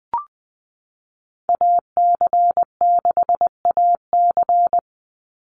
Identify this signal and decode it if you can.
Morse code, heard as a keyed tone, about 20 wpm, 700 Hz. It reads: AC6AC
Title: Call Signs Repeated 3x - 20wpm